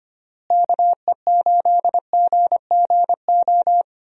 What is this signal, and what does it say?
Morse code, heard as a keyed tone, about 25 wpm, 700 Hz. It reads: KE8GGO
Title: USA Call Signs - 6 Characters - Repeated 2x - 25wpm